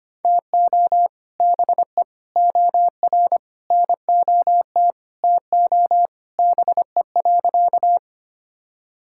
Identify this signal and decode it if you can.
Morse code, heard as a keyed tone, about 25 wpm, 700 Hz. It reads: TO BE OR NOT TO BE.
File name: practice / 25wpm / Sentences with 2-Word Phrases - 25wpm